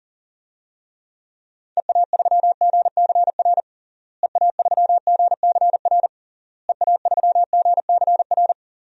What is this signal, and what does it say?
Morse code, heard as a keyed tone, about 37 wpm, 700 Hz. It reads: EA3GCR EA3GCR EA3GCR
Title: DXCC-100 Call Signs - 6 Characters - Repeated 3x - 40wpm